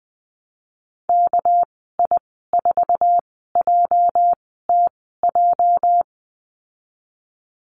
Morse code, heard as a keyed tone, about 20 wpm, spaced as for 15 wpm, 700 Hz. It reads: KI4JTJ
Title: 238 Lesson 38g - Letter q - Call Signs 20wpm